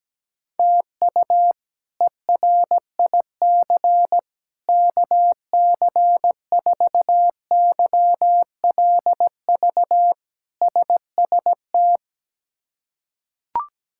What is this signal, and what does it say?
Morse code, heard as a keyed tone, about 17 wpm, 700 Hz. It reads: TU ERIC KC4YLV SST
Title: K1USN SST QSO Practice - 17wpm